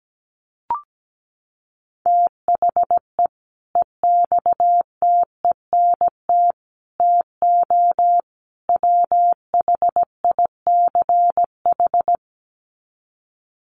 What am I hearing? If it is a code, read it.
Morse code, heard as a keyed tone, about 17 wpm, 700 Hz. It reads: THE EXTENT TO WHICH
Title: Sets of 4 Words 17wpm